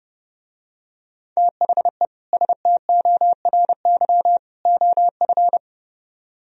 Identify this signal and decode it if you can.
Morse code, heard as a keyed tone, about 30 wpm, 700 Hz. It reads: THE STORY OF